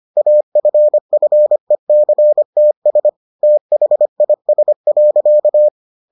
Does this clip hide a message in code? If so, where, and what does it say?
Yes: Morse code from 0.2 to 5.7 s, heard as a keyed tone, about 25 wpm, 600 Hz. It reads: AFFECTS THIS.